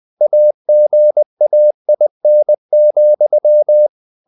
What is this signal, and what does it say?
Morse code, heard as a keyed tone, about 20 wpm, 600 Hz. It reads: AGAIN,
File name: the old man and the sea / the old man and the sea part 1 20wpm